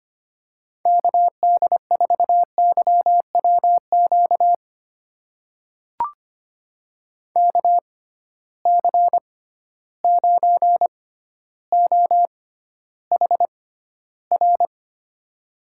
Morse code, heard as a keyed tone, about 25 wpm, spaced as for 13 wpm, 700 Hz. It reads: KD4YWQ KC9OHR
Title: USA Call Signs - 6 Characters - 6x Character Spacing - 25wpm